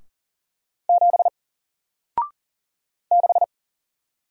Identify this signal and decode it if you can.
Morse code, heard as a keyed tone, about 40 wpm, 700 Hz. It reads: 7 6
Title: Single Character (Letter:Number:Punctuation:Prosign) Alt 40wpm